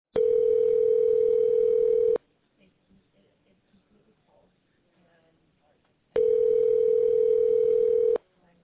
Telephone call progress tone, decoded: ringback tone

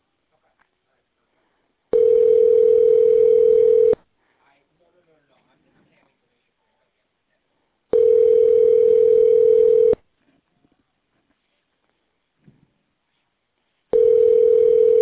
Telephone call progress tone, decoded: ringback tone

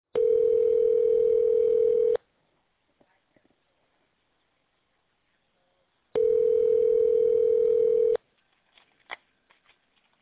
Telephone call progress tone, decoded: ringback tone

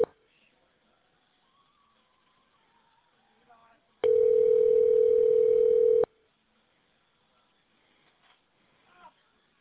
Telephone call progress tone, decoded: ringback tone

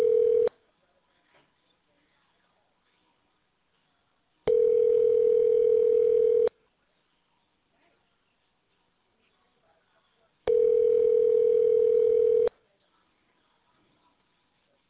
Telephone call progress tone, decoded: ringback tone

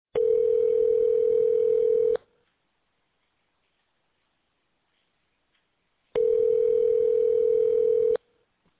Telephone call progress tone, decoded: ringback tone

